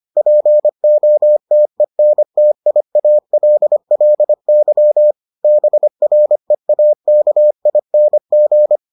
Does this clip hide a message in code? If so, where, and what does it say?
Yes: Morse code from 0.2 to 8.8 s, heard as a keyed tone, about 25 wpm, 600 Hz. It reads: POTENTIALLY BREAKING